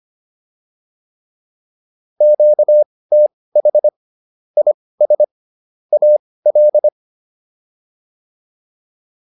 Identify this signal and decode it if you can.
Morse code, heard as a keyed tone, about 25 wpm, spaced as for 18 wpm, 600 Hz. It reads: QTH IS AL